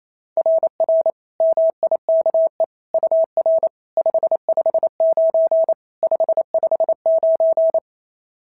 Morse code, heard as a keyed tone, about 27 wpm, 660 Hz. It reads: RR MIKE UR 559 559
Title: SOTA QSO Practice 28wpm